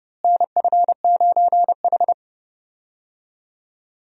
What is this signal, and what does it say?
Morse code, heard as a keyed tone, about 30 wpm, 700 Hz. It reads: NF9H
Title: USA Call Signs - 4 Characters - Repeated 2x - 30wpm